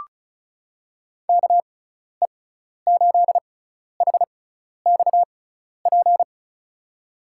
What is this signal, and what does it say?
Morse code, heard as a keyed tone, about 36 wpm, spaced as for 12 wpm, 690 Hz. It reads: KE8HXP
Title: USA Call Signs - 6 Characters - 6x Character Spacing - 35wpm